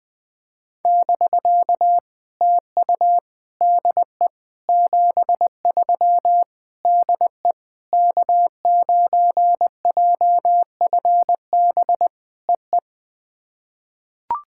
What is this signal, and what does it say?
Morse code, heard as a keyed tone, about 21 wpm, 700 Hz. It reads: <BK> TU DE 73 DE K9JFB EE
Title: POTA QSO Practice - 20wpm